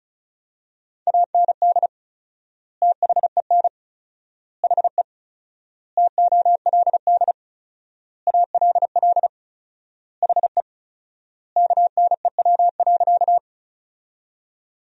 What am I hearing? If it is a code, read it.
Morse code, heard as a keyed tone, about 34 wpm, 700 Hz. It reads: AND THEN HE TOLD ALL HE KNEW.
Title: 4x Word-Spacing - Sentences from Top 400 Words - 35wpm